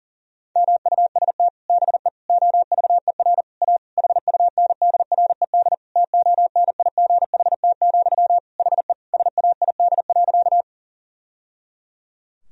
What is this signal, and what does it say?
Morse code, heard as a keyed tone, about 37 wpm, 700 Hz. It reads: MUST BE OVER A HUNDRED TONIGHT, HE SAID.
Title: Sentences from Top 1000 Words - Review Part 2 - 40wpm